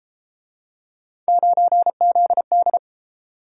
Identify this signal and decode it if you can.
Morse code, heard as a keyed tone, about 33 wpm, 700 Hz. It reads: 9ZD